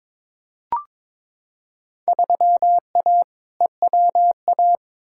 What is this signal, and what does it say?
Morse code, heard as a keyed tone, about 22 wpm, 700 Hz. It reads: 3A EWA
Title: Field Day Exchanges Encore - 22wpm